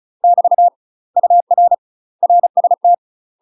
Morse code, heard as a keyed tone, about 36 wpm, 700 Hz. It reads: <BT> UR RST
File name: UK MASTER QSOs TRAINING / 35WPM / UK MASTER QSOs 0032